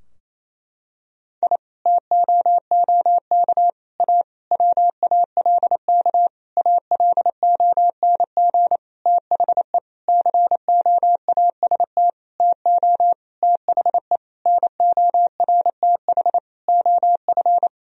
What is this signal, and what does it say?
Morse code, heard as a keyed tone, about 28 wpm, 700 Hz. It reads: I TOOK A WALK ALONG THE COAST TO THE NORTH OF
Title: Sentences from Top 1000 Words - Review Part 4 - 28wpm